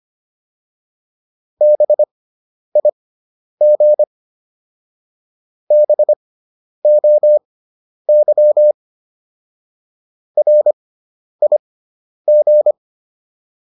Morse code, heard as a keyed tone, about 25 wpm, spaced as for 10 wpm, 600 Hz. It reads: BIG BOY RIG